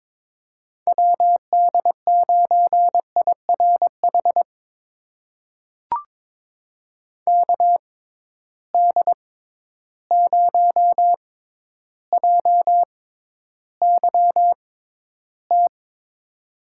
Morse code, heard as a keyed tone, about 21 wpm, spaced as for 11 wpm, 700 Hz. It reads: WD9IRH KD0JYT
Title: USA Call Signs - 6 Characters - 6x Character Spacing - 22wpm